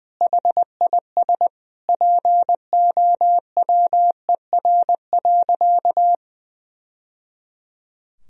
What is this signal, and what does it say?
Morse code, heard as a keyed tone, about 19 wpm, 700 Hz. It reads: HIS POWER.
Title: Sentences from Top 1000 Words - Review Part 2 - 20wpm